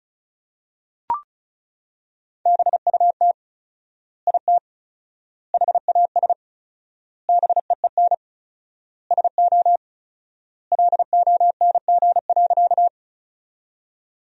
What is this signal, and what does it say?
Morse code, heard as a keyed tone, about 34 wpm, 700 Hz. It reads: BUT IT HAS BEEN SO LONG.
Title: 4x Word-Spacing - Sentences from Top 100 Words - 35wpm